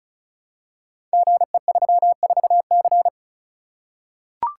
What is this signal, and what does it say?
Morse code, heard as a keyed tone, about 34 wpm, 700 Hz. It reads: GE34C
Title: Code Groups - 5 Letter:Numbers - 35wpm